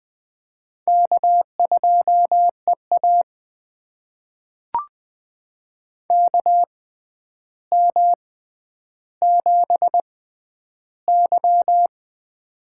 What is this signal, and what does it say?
Morse code, heard as a keyed tone, about 20 wpm, spaced as for 10 wpm, 700 Hz. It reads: K2EA KM7Y
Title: USA Call Signs - 4 Characters - 6x Character Spacing - 20wpm